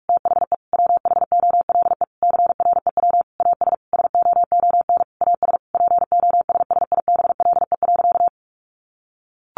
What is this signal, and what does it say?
Morse code, heard as a keyed tone, about 44 wpm, 700 Hz. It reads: THE WHOLE CREW AS SOON AS POSSIBLE.
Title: Sentences from Top 1500 Words 45wpm